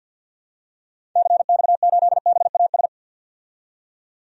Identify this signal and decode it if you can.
Morse code, heard as a keyed tone, about 45 wpm, 700 Hz. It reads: CX8BAS
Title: DXCC-100 Call Signs - 6 Characters - 50wpm